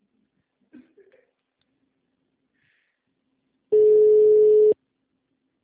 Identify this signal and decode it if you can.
Ringback tone (Europe, 425 Hz).